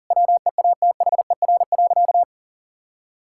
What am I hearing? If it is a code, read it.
Morse code, heard as a keyed tone, about 40 wpm, 700 Hz. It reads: WEATHER.
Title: Sentences from Top 1000 Words - Review Part 5 - 40wpm